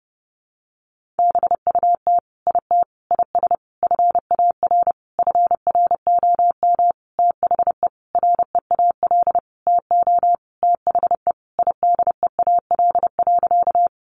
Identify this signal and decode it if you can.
Morse code, heard as a keyed tone, about 30 wpm, 700 Hz. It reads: BUT IT IS FAR FROM THE REAL TO THE IDEAL.
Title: Sentences from Top 1400 Words 30wpm